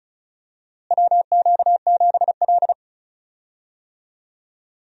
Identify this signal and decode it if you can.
Morse code, heard as a keyed tone, about 36 wpm, 700 Hz. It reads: WQ7L
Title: USA Call Signs - 4 Characters - Repeated 2x - 35wpm